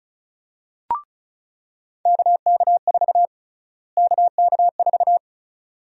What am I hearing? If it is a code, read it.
Morse code, heard as a keyed tone, about 36 wpm, 700 Hz. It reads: KK4 KK4
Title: USA Call Sign Prefixes - Repeated 2x - 35wpm